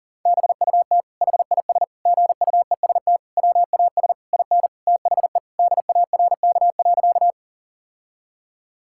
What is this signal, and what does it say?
Morse code, heard as a keyed tone, about 40 wpm, 700 Hz. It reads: BUT HIS GUEST WAS IN THE DARK.
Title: Sentences from Top 1600 Words 40wpm